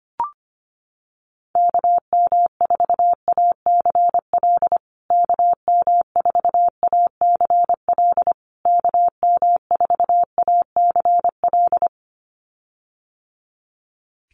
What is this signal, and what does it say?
Morse code, heard as a keyed tone, about 25 wpm, 700 Hz. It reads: KM4ACL KM4ACL KM4ACL
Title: USA Call Signs Repeated 3x 25wpm